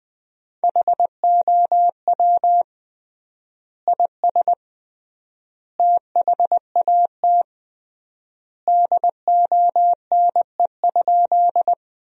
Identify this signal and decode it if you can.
Morse code, heard as a keyed tone, about 20 wpm, 700 Hz. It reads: HOW IS THAT DONE?
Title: 3x Word-Spacing - Sentences from Top 500 Words - 20wpm